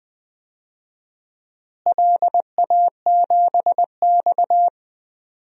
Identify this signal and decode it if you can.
Morse code, heard as a keyed tone, about 20 wpm, 700 Hz. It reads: LA7X
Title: DXCC-100 Call Signs - 4 Characters - Repeated 2x - 20wpm